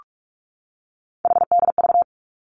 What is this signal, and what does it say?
Morse code, heard as a keyed tone, about 46 wpm, 700 Hz. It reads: HDV
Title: Three Challenging Letter-Number Combinations 45wpm